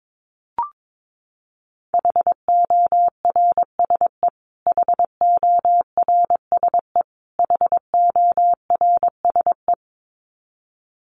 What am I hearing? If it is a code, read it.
Morse code, heard as a keyed tone, about 22 wpm, 700 Hz. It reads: HORSE HORSE HORSE
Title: Top 300 Words Repeated 22wpm